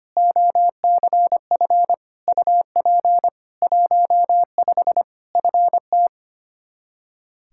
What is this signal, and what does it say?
Morse code, heard as a keyed tone, about 25 wpm, 700 Hz. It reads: OCF UP 15 FT
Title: K1USN SST QSO Practice 25wpm